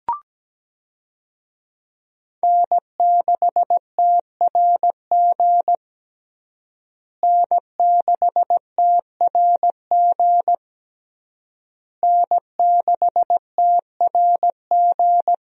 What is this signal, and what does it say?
Morse code, heard as a keyed tone, about 17 wpm, 700 Hz. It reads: N6TRG N6TRG N6TRG
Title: USA Call Signs - 5 Characters - Repeated 3x - 17wpm